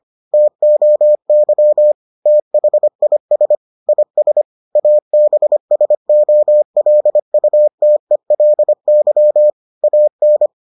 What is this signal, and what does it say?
Morse code, heard as a keyed tone, about 25 wpm, 600 Hz. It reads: TOY THIS IS ABSOLUTELY AN